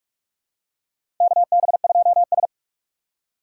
Morse code, heard as a keyed tone, about 42 wpm, 700 Hz. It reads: KB2S
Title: K1USN SST QSO Practice - 45wpm